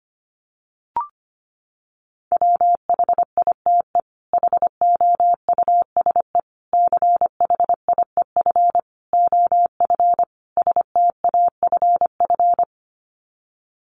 Morse code, heard as a keyed tone, about 25 wpm, 700 Hz. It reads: WHITE HOUSE CHIEF OF STAFF
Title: Sets of 5 Words Encore 25wpm